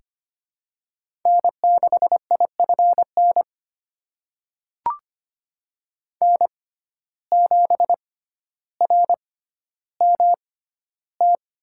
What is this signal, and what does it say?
Morse code, heard as a keyed tone, about 25 wpm, spaced as for 13 wpm, 690 Hz. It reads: N6IFN N7RMT
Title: USA Call Signs - 5 Characters - 6x Character Spacing - 25wpm